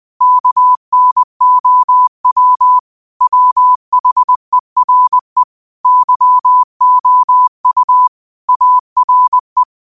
Morse code, heard as a keyed tone, about 20 wpm, 1000 Hz. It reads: KNOW WHERE YOU ARE
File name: qotdfiles / 20wpm